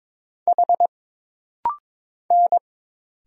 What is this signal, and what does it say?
Morse code, heard as a keyed tone, about 22 wpm, 700 Hz. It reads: H N
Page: This rapid-fire timing is geared to push your ICR (Instant Character Recognition) skills.